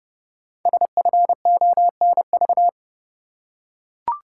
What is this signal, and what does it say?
Morse code, heard as a keyed tone, about 30 wpm, 700 Hz. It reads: SFONV
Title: Code Groups - 5 Letters - 30wpm